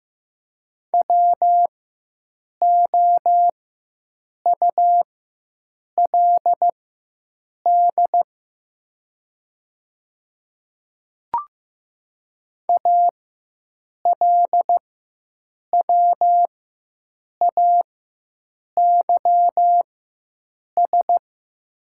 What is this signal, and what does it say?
Morse code, heard as a keyed tone, about 15 wpm, spaced as for 7 wpm, 700 Hz. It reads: WOULD ALWAYS
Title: 4x Character Spacing - Top 500 Words - 15wpm